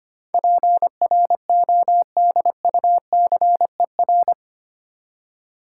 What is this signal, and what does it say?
Morse code, heard as a keyed tone, about 25 wpm, 700 Hz. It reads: PRODUCER
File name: Eight Letter Words - 25wpm